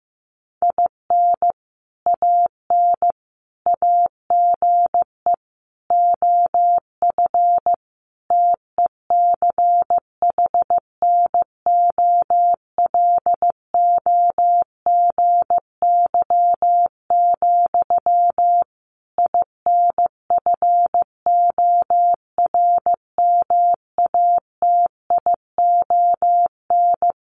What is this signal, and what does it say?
Morse code, heard as a keyed tone, about 15 wpm, 700 Hz. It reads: IN AN AGE OF TECHNOLOGY, INFORMATION
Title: Sentences from Top 1200 Words 15wpm